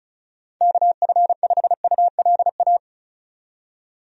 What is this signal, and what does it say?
Morse code, heard as a keyed tone, about 34 wpm, 700 Hz. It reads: KF5ULA
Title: USA Call Signs - 6 Characters - 35wpm